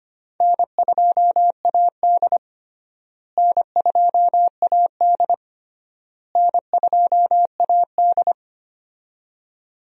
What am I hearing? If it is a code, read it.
Morse code, heard as a keyed tone, about 25 wpm, 700 Hz. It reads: N2AD N2AD N2AD
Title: USA Call Signs - 4 Characters - Repeated 3x - 25wpm